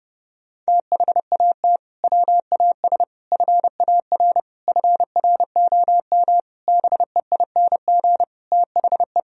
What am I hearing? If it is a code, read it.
Morse code, heard as a keyed tone, about 30 wpm, 700 Hz. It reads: THAT WAS FAR FROM BEING THE
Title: Sentences from Top 1000 Words - Review Part 3 - 30wpm